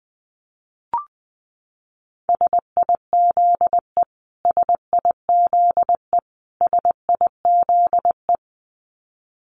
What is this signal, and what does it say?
Morse code, heard as a keyed tone, about 20 wpm, 700 Hz. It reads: SIZE SIZE SIZE
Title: Top 500 Words Repeated 20wpm